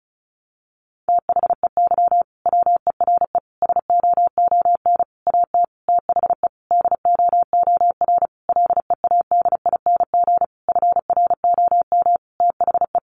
Morse code, heard as a keyed tone, about 34 wpm, 700 Hz. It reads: THEY WERE SOON AT THE DOOR LEADING FROM THE
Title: Sentences from Top 1200 Words 35wpm